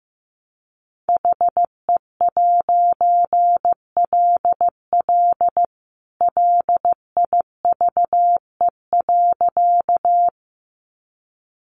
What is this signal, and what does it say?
Morse code, heard as a keyed tone, about 15 wpm, 700 Hz. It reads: HE'LL LIVE.
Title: A Sound Of Thunder 15wpm